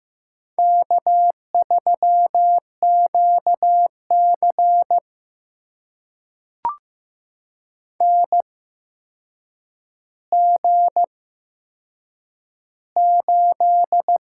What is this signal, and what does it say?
Morse code, heard as a keyed tone, about 15 wpm, spaced as for 7 wpm, 700 Hz. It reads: K3QC NG8
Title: USA Call Signs - 4 Characters - 8x Character Spacing - 15wpm